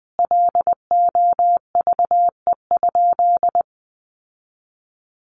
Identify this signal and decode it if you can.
Morse code, heard as a keyed tone, about 20 wpm, 700 Hz. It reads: LOVE?
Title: Sentences from Top 1500 Words 20wpm